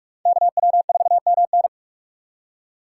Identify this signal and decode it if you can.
Morse code, heard as a keyed tone, about 43 wpm, 700 Hz. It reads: KW4MN